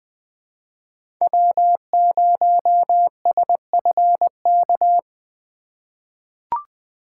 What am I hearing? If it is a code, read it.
Morse code, heard as a keyed tone, about 20 wpm, 700 Hz. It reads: W0SFK
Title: USA Call Signs - 4x Character Spacing - 20wpm